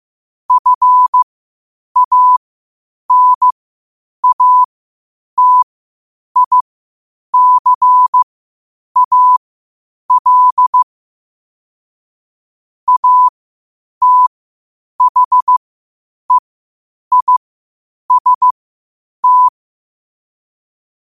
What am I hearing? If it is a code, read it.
Morse code, heard as a keyed tone, about 15 wpm, spaced as for 8 wpm, 1000 Hz. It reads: FANATICAL ATHEIST
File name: qotdfiles / 8wpm